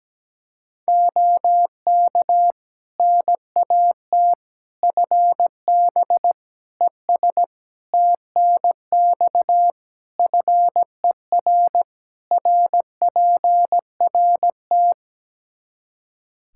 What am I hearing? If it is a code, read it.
Morse code, heard as a keyed tone, about 17 wpm, 700 Hz. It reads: OK NAT FB ES TNX FER RPRT
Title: Practice QSO - Exchange 2 - Info - 17wpm